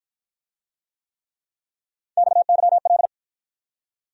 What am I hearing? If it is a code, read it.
Morse code, heard as a keyed tone, about 54 wpm, 700 Hz. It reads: XYL